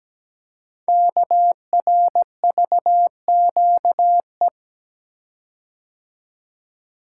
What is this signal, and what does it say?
Morse code, heard as a keyed tone, about 17 wpm, 700 Hz. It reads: KRVQE
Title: Code Groups - 5 Letters - 17wpm